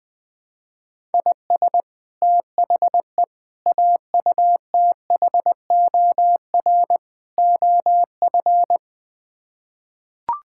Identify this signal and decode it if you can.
Morse code, heard as a keyed tone, about 20 wpm, 700 Hz. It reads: IS THE AUTHOR OF
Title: Build A Phrase - 4-Word Phrases - 20wpm